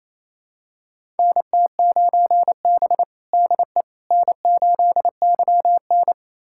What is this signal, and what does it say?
Morse code, heard as a keyed tone, about 28 wpm, 700 Hz. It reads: NT9B DE N8YN
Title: USA Callsign DE Callsign - 4 Characters - 28wpm